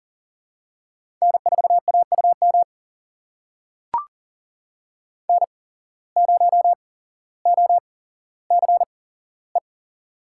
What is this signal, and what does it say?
Morse code, heard as a keyed tone, about 40 wpm, spaced as for 17 wpm, 700 Hz. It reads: N4AUM N0OCE
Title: USA Call Signs - 5 Characters - 8x Character Spacing - 40wpm